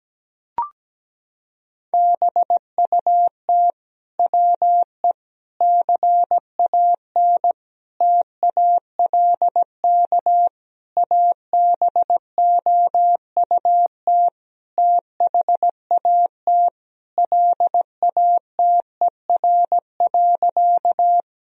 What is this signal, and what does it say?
Morse code, heard as a keyed tone, about 17 wpm, 700 Hz. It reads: BUT WE CAN TALK ABOUT THAT LATER.
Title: Sentences from Top 1000 Words - Review Part 4 - 17wpm